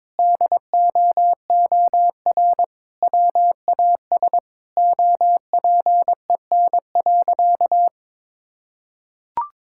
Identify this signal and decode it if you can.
Morse code, heard as a keyed tone, about 22 wpm, 700 Hz. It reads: DOOR WAS OPEN.
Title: Sentences from Top 1600 Words 22wpm